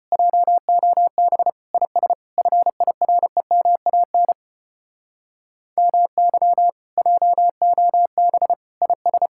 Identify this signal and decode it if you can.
Morse code, heard as a keyed tone, about 32 wpm, 700 Hz. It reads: JOB IS FIREMAN MY JOB IS